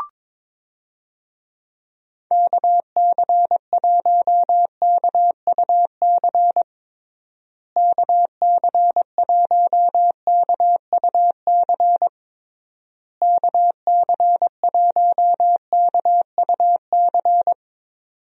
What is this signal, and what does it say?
Morse code, heard as a keyed tone, about 22 wpm, 700 Hz. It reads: KC1KUC KC1KUC KC1KUC
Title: USA Call Signs - 6 Characters - Repeated 3x - 22wpm